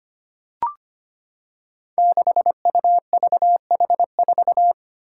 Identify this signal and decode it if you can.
Morse code, heard as a keyed tone, about 25 wpm, 700 Hz. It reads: 6UVH4
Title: Five Challenging Letter-Number Combinations 25wpm